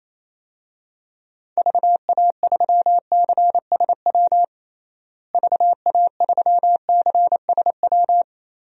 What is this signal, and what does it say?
Morse code, heard as a keyed tone, about 28 wpm, 700 Hz. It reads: VA3CSW VA3CSW
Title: DXCC-100 Call Signs - 6 Characters - Repeated 2x - 28wpm